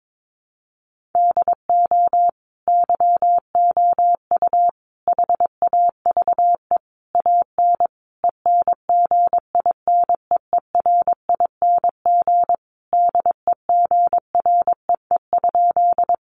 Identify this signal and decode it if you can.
Morse code, heard as a keyed tone, about 22 wpm, 700 Hz. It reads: DO YOU HAVE AN ENGINEERING DEGREE?
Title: Sentences from Top 1500 Words 22wpm